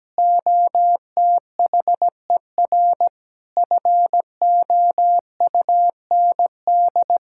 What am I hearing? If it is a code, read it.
Morse code, heard as a keyed tone, about 17 wpm, 700 Hz. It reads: OTHER FOUND
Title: Sentences from Top 1600 Words 17wpm